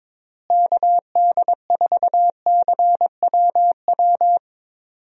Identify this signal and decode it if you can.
Morse code, heard as a keyed tone, about 22 wpm, 700 Hz. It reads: KD4CWW
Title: USA Call Signs - 6 Characters - Repeated 3x - 22wpm